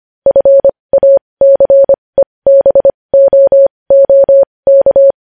Morse code, heard as a keyed tone, about 25 wpm, spaced as for 20 wpm, 550 Hz. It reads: FACEBOOK
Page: News Headlines in Morse Code at 20 WPM